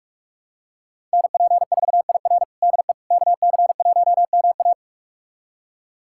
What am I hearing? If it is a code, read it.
Morse code, heard as a keyed tone, about 45 wpm, 700 Hz. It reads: NP4IR DE KC1MA
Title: USA Callsign DE Callsign - 5 Characters - 45wpm